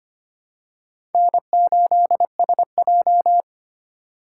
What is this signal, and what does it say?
Morse code, heard as a keyed tone, about 25 wpm, 700 Hz. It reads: N8SJ
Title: USA Call Signs - 4 Characters - 4x Character Spacing - 25wpm